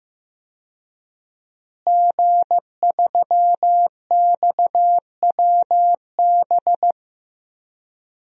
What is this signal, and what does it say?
Morse code, heard as a keyed tone, about 15 wpm, 700 Hz. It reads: G3XWB
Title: DXCC-100 Call Signs - 5 Characters - Repeated 3x - 15wpm